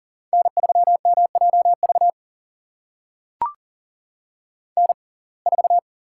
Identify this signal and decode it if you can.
Morse code, heard as a keyed tone, about 40 wpm, spaced as for 29 wpm, 690 Hz. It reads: N3MJV N4
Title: USA Call Signs - 5 Characters - 6x Character Spacing - 40wpm